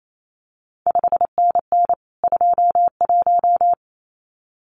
Morse code, heard as a keyed tone, about 28 wpm, 700 Hz. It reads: 5NN 21
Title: CQ WW DX Contest Exchange Encore - 28wpm